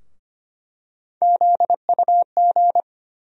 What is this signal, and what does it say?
Morse code, heard as a keyed tone, about 25 wpm, 700 Hz. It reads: ZUG
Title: USA Call Sign Postfixes - 3 Letters 25wpm